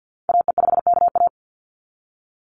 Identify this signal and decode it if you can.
Morse code, heard as a keyed tone, about 50 wpm, 700 Hz. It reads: AE5KA
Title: NA SOTA Activator Callsigns 50wpm